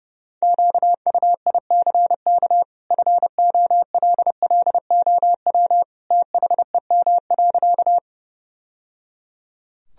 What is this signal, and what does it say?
Morse code, heard as a keyed tone, about 32 wpm, 700 Hz. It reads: QUICK FOLLOW THEM.
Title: Sentences from Top 1000 Words - Review Part 1 - 30wpm